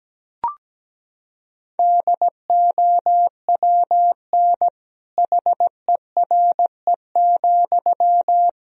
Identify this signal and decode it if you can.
Morse code, heard as a keyed tone, about 17 wpm, 700 Hz. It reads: DOWN HERE,
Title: Sentences from Top 1000 Words - Review Part 1 - 17wpm